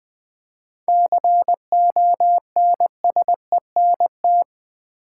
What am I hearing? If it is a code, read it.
Morse code, heard as a keyed tone, about 20 wpm, 700 Hz. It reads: CONSENT
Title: 7 Letter Words - First 3 Letters Given - 20wpm